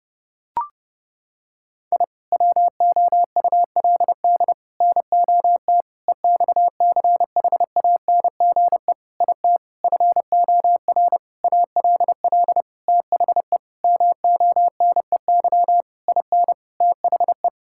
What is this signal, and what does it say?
Morse code, heard as a keyed tone, about 30 wpm, 700 Hz. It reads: I WOULD NOT EXCHANGE IT FOR ALL THE MONEY IN THE
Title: Sentences from Top 1000 Words - Review Part 6 - 30wpm